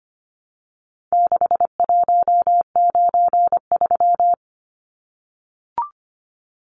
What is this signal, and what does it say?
Morse code, heard as a keyed tone, about 25 wpm, 700 Hz. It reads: 6193
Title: Numbers - 4-Digit 25wpm